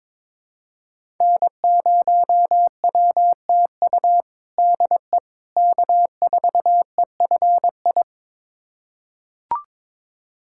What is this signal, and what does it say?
Morse code, heard as a keyed tone, about 22 wpm, 700 Hz. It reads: N0WTU DE K4EFI
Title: USA Callsign DE Callsign - 5 Characters - 22wpm